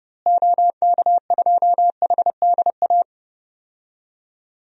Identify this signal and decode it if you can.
Morse code, heard as a keyed tone, about 32 wpm, 700 Hz. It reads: OK2HDA
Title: DXCC-100 Call Signs - 6 Characters - Repeated 3x - 30wpm